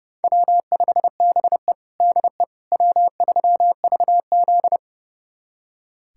Morse code, heard as a keyed tone, about 30 wpm, 700 Hz. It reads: W5BE DE W3VZ
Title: USA Callsign DE Callsign - 4 Characters - 30wpm